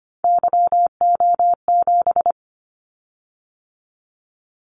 Morse code, heard as a keyed tone, about 25 wpm, 700 Hz. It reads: YO7
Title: Callsign Prefixes for Top 25 DXCC - 25wpm